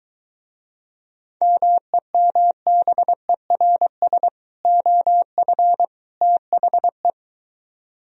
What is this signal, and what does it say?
Morse code, heard as a keyed tone, about 23 wpm, 700 Hz. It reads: MEMBERS OF THE